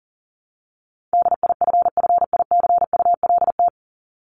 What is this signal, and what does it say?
Morse code, heard as a keyed tone, about 40 wpm, 700 Hz. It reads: DIFFICULT
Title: Top 501-750 Words 40wpm